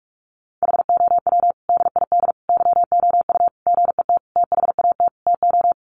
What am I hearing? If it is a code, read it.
Morse code, heard as a keyed tone, about 44 wpm, 700 Hz. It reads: HOW DID YOU GET THAT TO
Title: Sentences from Top 900 Words 45wpm